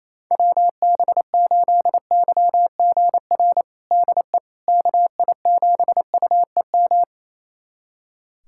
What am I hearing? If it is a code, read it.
Morse code, heard as a keyed tone, about 28 wpm, 700 Hz. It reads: WB8YGR DE KI7UEM
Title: USA Callsign DE Callsign - 6 Characters - 28wpm